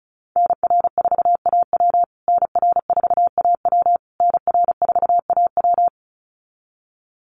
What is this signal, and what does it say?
Morse code, heard as a keyed tone, about 34 wpm, 700 Hz. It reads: NR4AW NR4AW NR4AW
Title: USA Call Signs Repeated 3x 35wpm